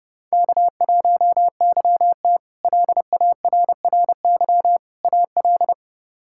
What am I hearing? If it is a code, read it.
Morse code, heard as a keyed tone, about 30 wpm, 700 Hz. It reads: K1YT LARRY AL
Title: K1USN SST QSO Practice - 30wpm